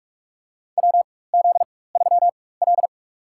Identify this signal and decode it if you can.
Morse code, heard as a keyed tone, about 43 wpm, spaced as for 21 wpm, 700 Hz. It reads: WZ3L
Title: USA Call Signs - 4 Characters - 4x Character Spacing - 45wpm